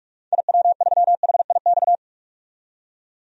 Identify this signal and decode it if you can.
Morse code, heard as a keyed tone, about 42 wpm, 700 Hz. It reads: IW3HIX
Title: DXCC-100 Call Signs - 6 Characters - 45wpm